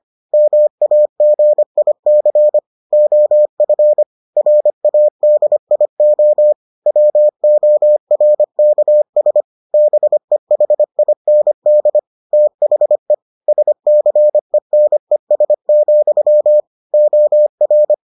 Morse code, heard as a keyed tone, about 25 wpm, 600 Hz. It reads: MAGIC OF RADIO WORKS BEHIND THE SCENES, OR